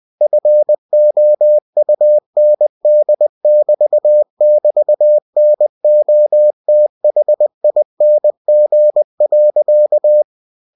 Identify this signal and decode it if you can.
Morse code, heard as a keyed tone, about 20 wpm, 600 Hz. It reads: FOUND<BT><BT>NOTHING.
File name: cw-books / winnie-the-pooh / winnie-20wpm-page-0039